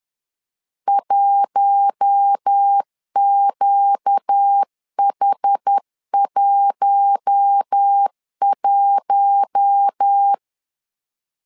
Morse code, heard as a keyed tone, about 10 wpm, 800 Hz. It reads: 1QH11
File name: LEZIONI / 4 Q1 10wpm 2448